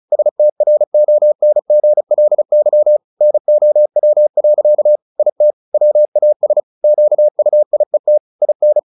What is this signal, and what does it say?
Morse code, heard as a keyed tone, about 34 wpm, 600 Hz. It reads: STRONGLY NOW. IT WAS QUIET IN
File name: the old man and the sea 2 35wpm